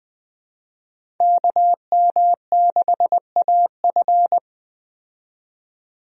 Morse code, heard as a keyed tone, about 20 wpm, 700 Hz. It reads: KM6AF
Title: USA Call Signs - 5 Characters - 20wpm